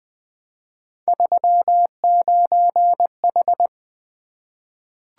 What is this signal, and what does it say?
Morse code, heard as a keyed tone, about 20 wpm, 700 Hz. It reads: 39H
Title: Three Letter-Number Combinations - 20wpm